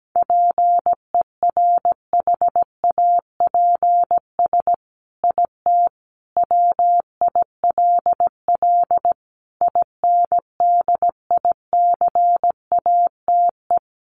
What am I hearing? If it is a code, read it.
Morse code, heard as a keyed tone, about 17 wpm, 700 Hz. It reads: PERHAPS IT WILL INDICATE
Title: Sentences from Top 1300 Words 17wpm